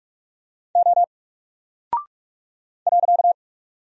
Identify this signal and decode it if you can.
Morse code, heard as a keyed tone, about 44 wpm, 700 Hz. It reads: O .
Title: Single Character (Letter:Number:Punctuation:Prosign) Alt 45wpm